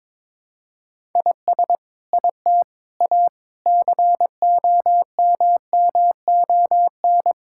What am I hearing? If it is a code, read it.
Morse code, heard as a keyed tone, about 22 wpm, 700 Hz. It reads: IS IT A COMMON
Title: Sentences from Top 1000 Words - Review Part 5 - 22wpm